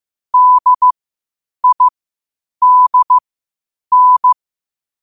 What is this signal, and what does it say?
Morse code, heard as a keyed tone, about 15 wpm, spaced as for 8 wpm, 1000 Hz. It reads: DIDN
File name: qotdfiles / 8wpm